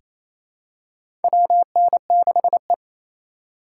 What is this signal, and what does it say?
Morse code, heard as a keyed tone, about 28 wpm, 700 Hz. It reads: WN6E
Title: USA Call Signs - 4 Characters - 3x Character Spacing - 28wpm